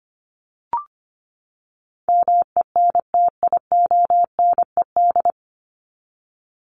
Morse code, heard as a keyed tone, about 25 wpm, 700 Hz. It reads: MENTIONED
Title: ED Suffix - Top 100 Words with Base Words 25wpm